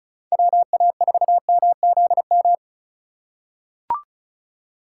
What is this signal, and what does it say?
Morse code, heard as a keyed tone, about 34 wpm, 700 Hz. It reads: WA4MZM
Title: USA Call Signs - 6 Characters - 2x Character Spacing - 35wpm